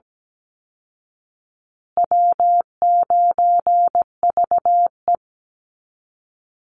Morse code, heard as a keyed tone, about 17 wpm, 700 Hz. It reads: W9VE
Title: Easy Call Signs Repeated 3x - 17wpm